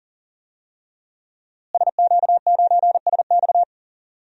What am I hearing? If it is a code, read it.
Morse code, heard as a keyed tone, about 40 wpm, 700 Hz. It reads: SQ9SX
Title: DXCC-100 Call Signs - 5 Characters - Repeated 3x - 40wpm